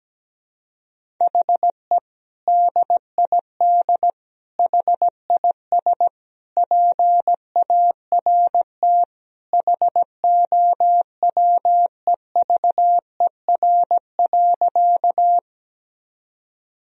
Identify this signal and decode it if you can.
Morse code, heard as a keyed tone, about 17 wpm, 700 Hz. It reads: HE DID HIS PART HOWEVER.
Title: Sentences from Top 1000 Words - Review Part 1 - 17wpm